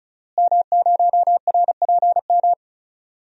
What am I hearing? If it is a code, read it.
Morse code, heard as a keyed tone, about 35 wpm, 700 Hz. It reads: M0RPM
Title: DXCC-100 Call Signs - 5 Characters - Repeated 3x - 35wpm